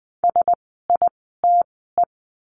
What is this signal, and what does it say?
Morse code, heard as a keyed tone, about 20 wpm, spaced as for 14 wpm, 700 Hz. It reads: SITE
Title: 054 Lesson 10f - Letter r - Words 20wpm